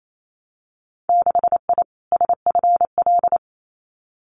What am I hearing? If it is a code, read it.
Morse code, heard as a keyed tone, about 28 wpm, 690 Hz. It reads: 6I SFL
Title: Winter Field Day Exchanges Encore 28wpm